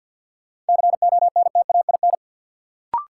Morse code, heard as a keyed tone, about 52 wpm, 700 Hz. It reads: CONTAIN
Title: Sentences from Top 500 Words - New Word Drill - 50wpm